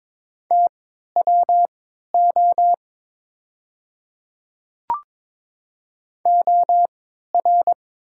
Morse code, heard as a keyed tone, about 23 wpm, spaced as for 13 wpm, 700 Hz. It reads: TWO OR
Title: 3x Character Spacing - Top 100 Words - 22wpm